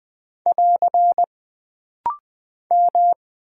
Morse code, heard as a keyed tone, about 20 wpm, 700 Hz. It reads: <AR> M
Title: Single Character (Letter:Number:Punctuation:Prosign) - Rapid-Fire - 20wpm